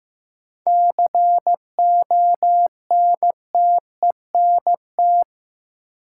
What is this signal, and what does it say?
Morse code, heard as a keyed tone, about 15 wpm, 700 Hz. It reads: CONTENT
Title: Sentences from Top 900 Words - New Word Drill - 15wpm